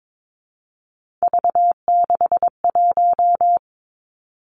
Morse code, heard as a keyed tone, about 22 wpm, 700 Hz. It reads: V61
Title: Three Challenging Letter-Number Combinations 22wpm